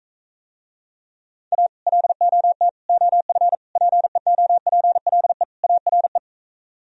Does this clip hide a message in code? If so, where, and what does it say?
Yes: Morse code from 1.5 to 6.2 s, heard as a keyed tone, about 44 wpm, 700 Hz. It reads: A LOT OF PEOPLE ARE